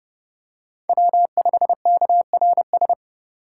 Morse code, heard as a keyed tone, about 30 wpm, 700 Hz. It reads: W5KRS
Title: USA Call Signs - 4x Character Spacing - 30wpm